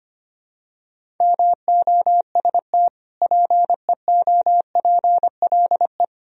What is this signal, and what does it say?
Morse code, heard as a keyed tone, about 25 wpm, 700 Hz. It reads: MOST PEOPLE
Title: Sentences from Top 1000 Words - Review Part 1 - 25wpm